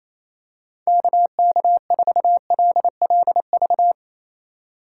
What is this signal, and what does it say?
Morse code, heard as a keyed tone, about 28 wpm, 700 Hz. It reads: KK4LLV
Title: USA Call Signs - 6 Characters - 2x Character Spacing - 28wpm